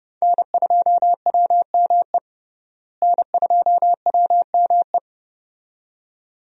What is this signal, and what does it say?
Morse code, heard as a keyed tone, about 31 wpm, 700 Hz. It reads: N2WME N2WME
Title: USA Call Signs - 5 Characters - Repeated 2x - 30wpm